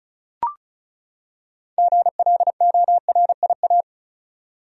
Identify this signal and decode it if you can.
Morse code, heard as a keyed tone, about 34 wpm, 700 Hz. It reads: GLORIA
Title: USA Common Names - 35wpm